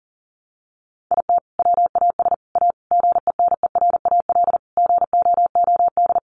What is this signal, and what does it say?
Morse code, heard as a keyed tone, about 40 wpm, 700 Hz. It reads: IT WAS A GENERAL GOOD
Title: Sentences from Top 600 Words 40wpm